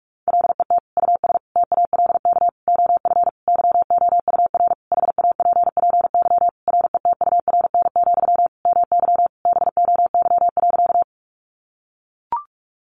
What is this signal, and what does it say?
Morse code, heard as a keyed tone, about 44 wpm, 700 Hz. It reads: LET US TALK OF YOUR HAPPY RETURN, MY BOY.
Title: Sentences from Top 1000 Words 45wpm